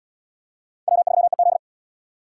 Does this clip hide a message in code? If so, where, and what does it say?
Yes: Morse code from 0.9 to 1.6 s, heard as a keyed tone, about 75 wpm, 700 Hz. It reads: W3EZ